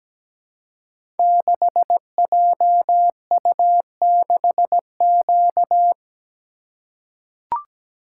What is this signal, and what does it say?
Morse code, heard as a keyed tone, about 17 wpm, 700 Hz. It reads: 6JU6Q
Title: Code Groups - 5 Letter:Numbers - 17wpm